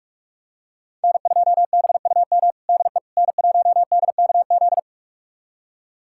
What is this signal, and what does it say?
Morse code, heard as a keyed tone, about 46 wpm, 700 Hz. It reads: N2BUM DE N1DKZ
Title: USA Callsign DE Callsign - 5 Characters - 45wpm